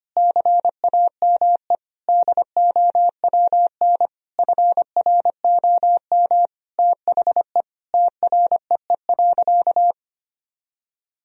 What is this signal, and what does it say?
Morse code, heard as a keyed tone, about 25 wpm, 700 Hz. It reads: CAME DOWN FROM THE TREE.
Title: Sentences from Top 1000 Words - Review Part 1 - 25wpm